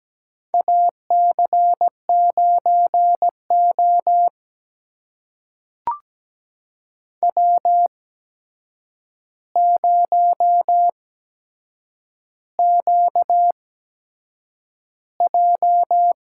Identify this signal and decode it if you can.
Morse code, heard as a keyed tone, about 17 wpm, spaced as for 7 wpm, 700 Hz. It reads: AC9O W0QJ
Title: USA Call Signs - 4 Characters - 8x Character Spacing - 17wpm